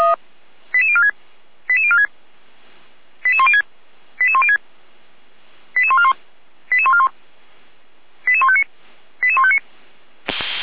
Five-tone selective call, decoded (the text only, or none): 88046 88046 88106 88106 88141 88141 88159 88159